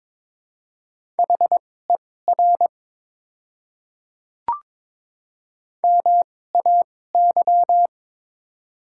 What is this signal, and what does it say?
Morse code, heard as a keyed tone, about 22 wpm, spaced as for 16 wpm, 700 Hz. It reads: HER MAY
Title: 2x Character Spacing - Top 100 Words - 22wpm